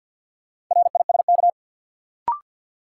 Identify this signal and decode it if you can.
Morse code, heard as a keyed tone, about 52 wpm, 700 Hz. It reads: RISK
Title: Sentences from Top 600 Words - New Word Drill - 50wpm